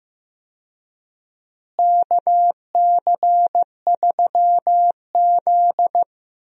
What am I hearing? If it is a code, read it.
Morse code, heard as a keyed tone, about 15 wpm, 700 Hz. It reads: KC3Z
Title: USA Call Signs - 4 Characters - Repeated 3x - 15wpm